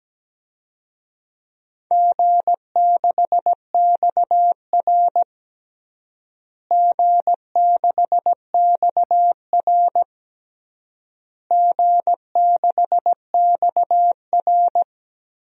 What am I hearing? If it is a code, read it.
Morse code, heard as a keyed tone, about 17 wpm, 700 Hz. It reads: G6XR G6XR G6XR
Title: DXCC-100 Call Signs - 4 Characters - Repeated 3x - 17wpm